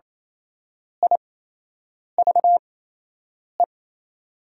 Morse code, heard as a keyed tone, about 28 wpm, spaced as for 8 wpm, 700 Hz. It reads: I VE
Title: 8x Character Spacing - Top 200 Words - 28wpm